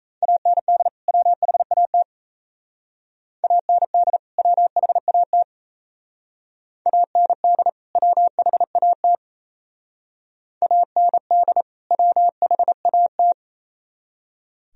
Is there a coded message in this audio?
Yes: Morse code from 0.2 to 13.3 s, heard as a keyed tone, about 36 wpm, 700 Hz. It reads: AND WHAT AND WHAT AND WHAT AND WHAT